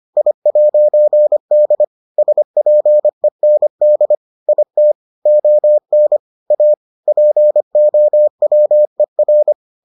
Morse code, heard as a keyed tone, about 25 wpm, 600 Hz. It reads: I'D SPEND IT ON A POWER